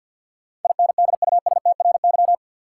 Morse code, heard as a keyed tone, about 51 wpm, 700 Hz. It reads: INDUSTRY
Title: Sentences from Top 1000 Words - Review Part 4 - New Word Drill - 50wpm